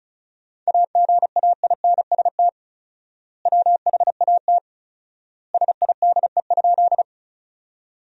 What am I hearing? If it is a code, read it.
Morse code, heard as a keyed tone, about 36 wpm, 700 Hz. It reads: AGAINST WHAT SIDE?
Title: 4x Word-Spacing - Sentences from Top 400 Words - 35wpm